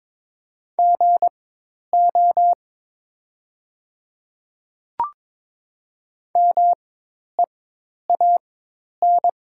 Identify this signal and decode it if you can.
Morse code, heard as a keyed tone, about 22 wpm, spaced as for 10 wpm, 700 Hz. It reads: GO MEAN
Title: 4x Character Spacing - Top 300 Words - 22wpm